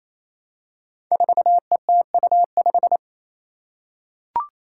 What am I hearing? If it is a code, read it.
Morse code, heard as a keyed tone, about 29 wpm, 700 Hz. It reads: 4ETU5
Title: Five Challenging Letter-Number Combinations 28wpm